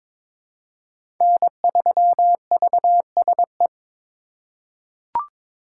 Morse code, heard as a keyed tone, about 21 wpm, 700 Hz. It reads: N3VSE
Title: USA Call Signs - 5 Characters - 3x Character Spacing - 22wpm